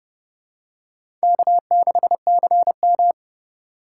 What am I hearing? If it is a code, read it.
Morse code, heard as a keyed tone, about 32 wpm, 700 Hz. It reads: K6CM
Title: USA Call Signs - 4 Characters - 4x Character Spacing - 30wpm